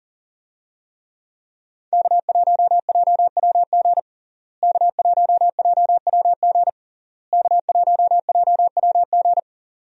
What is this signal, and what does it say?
Morse code, heard as a keyed tone, about 37 wpm, 700 Hz. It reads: K1JWG K1JWG K1JWG
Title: USA Call Signs - 5 Characters - Repeated 3x - 40wpm